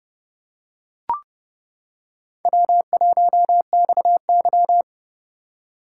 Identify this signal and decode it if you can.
Morse code, heard as a keyed tone, about 24 wpm, 700 Hz. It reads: W1XY
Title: Speed Racing - USA 1x2 Call Signs 20wpm